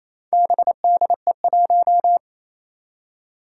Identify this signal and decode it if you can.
Morse code, heard as a keyed tone, about 27 wpm, 700 Hz. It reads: BDE1
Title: Four Challenging Letter-Number Combinations 28wpm